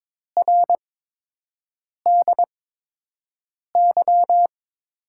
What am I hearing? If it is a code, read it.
Morse code, heard as a keyed tone, about 22 wpm, spaced as for 6 wpm, 700 Hz. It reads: RDY
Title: USA Call Signs - 5 Characters - 8x Character Spacing - 22wpm